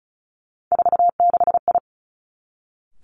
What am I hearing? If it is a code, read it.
Morse code, heard as a keyed tone, about 36 wpm, 700 Hz. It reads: 46I
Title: Three Challenging Letter-Number Combinations 35wpm